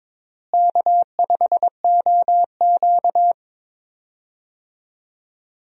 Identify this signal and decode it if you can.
Morse code, heard as a keyed tone, about 22 wpm, 700 Hz. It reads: K5OQ
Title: USA Call Signs - 4 Characters - Repeated 2x - 22wpm